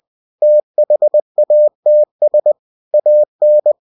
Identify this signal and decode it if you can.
Morse code, heard as a keyed tone, about 20 wpm, 600 Hz. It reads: THATS AN
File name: cw-books / winnie-the-pooh / winnie-20wpm-page-0082